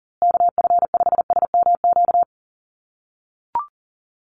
Morse code, heard as a keyed tone, about 40 wpm, 700 Hz. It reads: KF5SMQ
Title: USA Call Signs Repeated 3x 40wpm